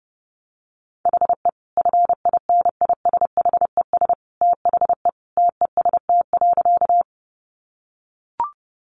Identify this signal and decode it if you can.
Morse code, heard as a keyed tone, about 30 wpm, 700 Hz. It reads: HE FINISHES THE TEST.
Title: Additional Easy Phrases 30wpm